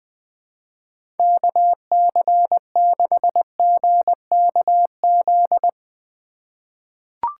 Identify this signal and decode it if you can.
Morse code, heard as a keyed tone, about 20 wpm, 700 Hz. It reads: KC6GKZ
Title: USA Call Signs - 6 Characters - 4x Character Spacing - 20wpm